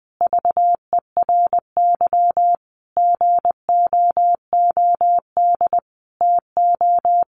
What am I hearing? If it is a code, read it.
Morse code, heard as a keyed tone, about 20 wpm, 700 Hz. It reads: VERY GOOD TO
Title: Sentences from Top 200 Words 20wpm